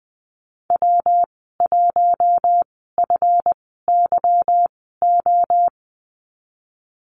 Morse code, heard as a keyed tone, about 20 wpm, spaced as for 14 wpm, 700 Hz. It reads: W1FYO
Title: 186 Lesson 30g - Letter b - Call Signs 20wpm